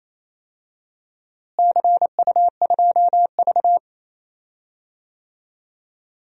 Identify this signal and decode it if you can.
Morse code, heard as a keyed tone, about 27 wpm, 700 Hz. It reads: CU2V
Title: DXCC-100 Call Signs - 4 Characters - Repeated 2x - 28wpm